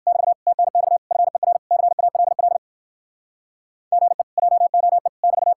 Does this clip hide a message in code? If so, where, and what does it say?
Yes: Morse code from 0.1 to 5.5 s, heard as a keyed tone, about 58 wpm, 700 Hz. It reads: <BT> TNX FER CALL GE JOE <BT>